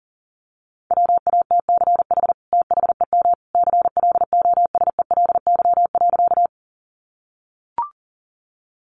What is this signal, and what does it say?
Morse code, heard as a keyed tone, about 40 wpm, 700 Hz. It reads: WATCH THEM CLOSELY.
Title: Sentences from Top 1400 Words 40wpm